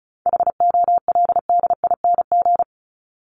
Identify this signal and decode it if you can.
Morse code, heard as a keyed tone, about 34 wpm, 700 Hz. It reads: HOLDING
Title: ING Suffix - Top 100 Words with Base Words 35wpm